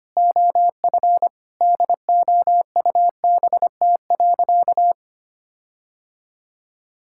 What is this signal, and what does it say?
Morse code, heard as a keyed tone, about 25 wpm, 700 Hz. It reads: OF DOUBT.
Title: Sentences from Top 1600 Words 25wpm